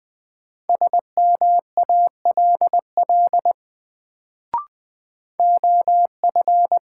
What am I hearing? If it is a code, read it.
Morse code, heard as a keyed tone, about 20 wpm, 700 Hz. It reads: SMALL OF
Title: Sets of 2 Words - New Word Drill - 20wpm